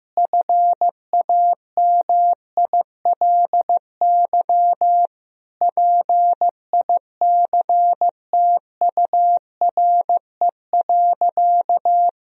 Morse code, heard as a keyed tone, about 15 wpm, 700 Hz. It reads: FAMILY PICTURE.
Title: Sentences from Top 1000 Words - Review Part 3 - 15wpm